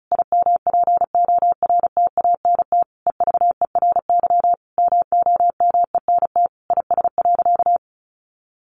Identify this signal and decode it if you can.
Morse code, heard as a keyed tone, about 34 wpm, 700 Hz. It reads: IMPORTANT EVERY MOMENT IS.
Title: Sentences from Top 1000 Words 35wpm